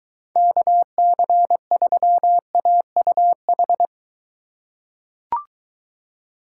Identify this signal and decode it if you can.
Morse code, heard as a keyed tone, about 20 wpm, 700 Hz. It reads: KC3AUH